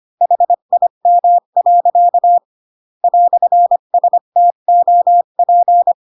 Morse code, heard as a keyed tone, about 25 wpm, 700 Hz. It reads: HIM. "STOP
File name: Pinocchio / Pinocchio ch3 20 25 ik1hge